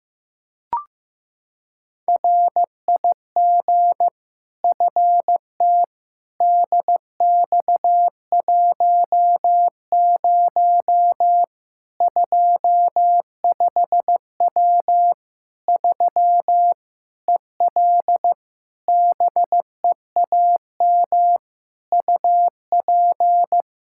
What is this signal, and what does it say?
Morse code, heard as a keyed tone, about 15 wpm, 700 Hz. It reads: RIG FTDX10 25W 3 EL BEAM UP
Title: Practice QSO - Exchange 2 - Info - 15wpm